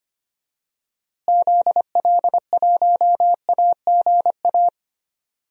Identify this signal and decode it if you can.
Morse code, heard as a keyed tone, about 25 wpm, 700 Hz. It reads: ZL1AGA
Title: DXCC-100 Call Signs - 6 Characters - Repeated 3x - 25wpm